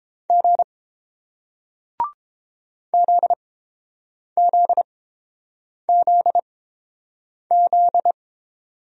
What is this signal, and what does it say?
Morse code, heard as a keyed tone, about 30 wpm, spaced as for 7 wpm, 700 Hz. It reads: G ZZZZ